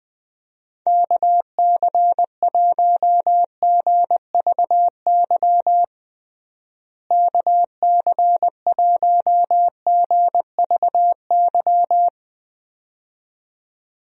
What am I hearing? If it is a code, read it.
Morse code, heard as a keyed tone, about 20 wpm, 700 Hz. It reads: KC1GVY KC1GVY
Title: USA Call Signs - 6 Characters - Repeated 2x - 20wpm